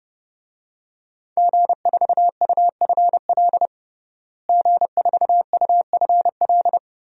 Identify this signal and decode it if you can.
Morse code, heard as a keyed tone, about 32 wpm, 700 Hz. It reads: G4UFL G4UFL
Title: DXCC-100 Call Signs - 5 Characters - Repeated 3x - 30wpm